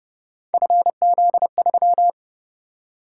Morse code, heard as a keyed tone, about 31 wpm, 700 Hz. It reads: FZ3
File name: Speed-Racing - Three Letter-Number Combinations - 25wpm